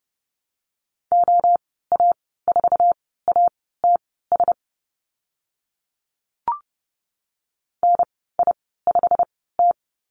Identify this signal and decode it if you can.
Morse code, heard as a keyed tone, about 30 wpm, spaced as for 17 wpm, 700 Hz. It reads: OA4ATS NI5T
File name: Lesson 15g - Number 5 - Call Signs 30wpm